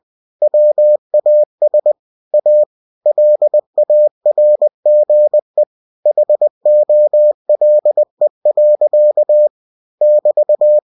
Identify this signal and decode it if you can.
Morse code, heard as a keyed tone, about 20 wpm, 600 Hz. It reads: WAS A LARGE HOLE. <BT>
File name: cw-books / winnie-the-pooh / winnie-20wpm-page-0013